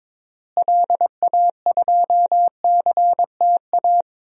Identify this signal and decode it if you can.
Morse code, heard as a keyed tone, about 22 wpm, 700 Hz. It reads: LA2CTA
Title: DXCC-100 Call Signs - 6 Characters - Repeated 2x - 22wpm